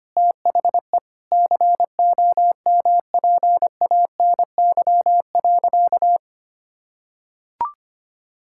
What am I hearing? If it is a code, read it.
Morse code, heard as a keyed tone, about 25 wpm, 700 Hz. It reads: THE COMPANY.
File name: practice / 25wpm / Sentences with 3-Word Phrases - 25wpm